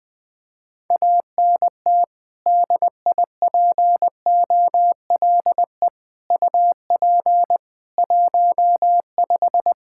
Morse code, heard as a keyed tone, about 20 wpm, 700 Hz. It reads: ANT DIPOLE UP 15
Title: Practice QSO - Exchange 2 - Info - 20wpm